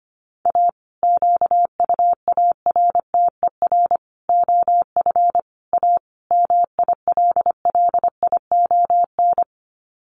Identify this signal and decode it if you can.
Morse code, heard as a keyed tone, about 25 wpm, 700 Hz. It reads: A QUARTER OF A MILLION
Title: Sets of 5 Words Encore2 25wpm